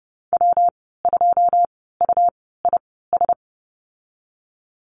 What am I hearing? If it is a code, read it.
Morse code, heard as a keyed tone, about 30 wpm, spaced as for 17 wpm, 700 Hz. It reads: W2UIS
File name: Lesson 19g - Letter w - Call Signs 30wpm